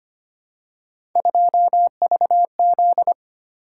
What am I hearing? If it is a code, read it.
Morse code, heard as a keyed tone, about 25 wpm, 700 Hz. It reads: 2VZ
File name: Three Letter-Number Combinations - 25wpm